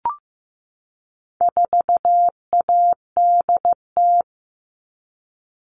Morse code, heard as a keyed tone, about 15 wpm, 700 Hz. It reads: 4ADT
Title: All Cut Numbers - 4-Digit 15wpm